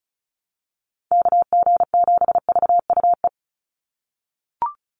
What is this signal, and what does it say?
Morse code, heard as a keyed tone, about 34 wpm, 700 Hz. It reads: KG7VUE
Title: USA Call Signs 35wpm